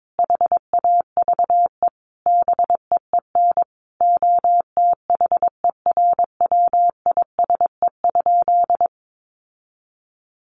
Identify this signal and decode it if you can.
Morse code, heard as a keyed tone, about 22 wpm, 700 Hz. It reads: HAVE BEEN OTHERWISE?
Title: Sentences from Top 1200 Words 22wpm